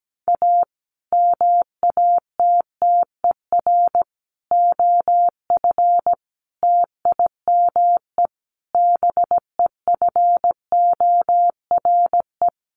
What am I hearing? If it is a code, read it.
Morse code, heard as a keyed tone, about 17 wpm, 700 Hz. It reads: A MATTER OF TIME BEFORE
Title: Sets of 5 Words Encore 17wpm